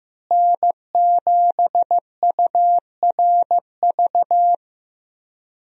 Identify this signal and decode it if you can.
Morse code, heard as a keyed tone, about 15 wpm, 700 Hz. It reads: N7URV
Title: USA Call Signs - 5 Characters - Repeated 3x - 15wpm